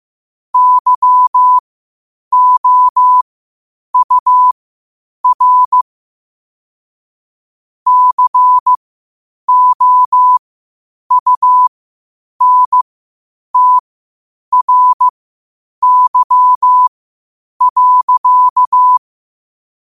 Morse code, heard as a keyed tone, about 15 wpm, spaced as for 8 wpm, 1000 Hz. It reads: YOUR COUNTRY.
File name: qotdfiles / 8wpm